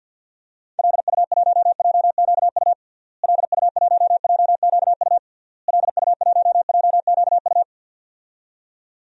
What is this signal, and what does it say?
Morse code, heard as a keyed tone, about 53 wpm, 700 Hz. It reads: LU1JQU LU1JQU LU1JQU
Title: DXCC-100 Call Signs - 6 Characters - Repeated 3x - 50wpm